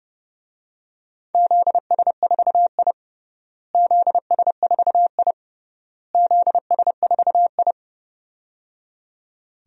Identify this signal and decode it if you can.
Morse code, heard as a keyed tone, about 30 wpm, 700 Hz. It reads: ZS4I ZS4I ZS4I
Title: DXCC-100 Call Signs - 4 Characters - Repeated 3x - 30wpm